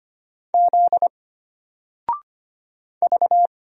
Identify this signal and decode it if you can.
Morse code, heard as a keyed tone, about 25 wpm, 700 Hz. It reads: Z V
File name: Single-Letter-Number-25wpm